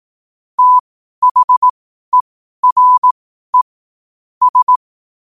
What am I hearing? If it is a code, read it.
Morse code, heard as a keyed tone, about 18 wpm, spaced as for 12 wpm, 1000 Hz. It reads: THERE S
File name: qotdfiles / 12wpm